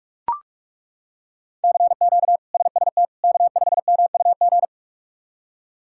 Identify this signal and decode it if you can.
Morse code, heard as a keyed tone, about 42 wpm, 700 Hz. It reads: CQ SST K5MUG
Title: K1USN SST QSO Practice - 45wpm